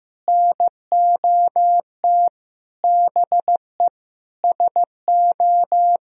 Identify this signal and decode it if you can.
Morse code, heard as a keyed tone, about 15 wpm, 700 Hz. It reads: NOT BE SO
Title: Sentences from Top 1000 Words - Review Part 1 - 15wpm